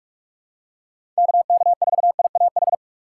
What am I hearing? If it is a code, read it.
Morse code, heard as a keyed tone, about 47 wpm, 700 Hz. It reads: KK4IAH
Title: USA Call Signs - 6 Characters - 3x Character Spacing - 45wpm